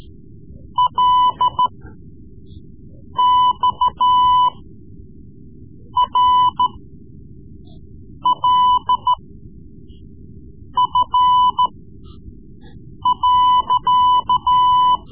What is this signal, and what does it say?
Morse code, heard as a keyed tone, about 13 wpm, spaced as for 5 wpm, 1000 Hz. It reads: LXRLF.
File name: Morse Lessons / Morse Lessons / Tape-3B